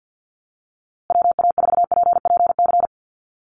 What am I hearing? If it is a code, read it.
Morse code, heard as a keyed tone, about 50 wpm, 700 Hz. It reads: WA4PPC
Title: USA Call Signs Repeated 2x 50wpm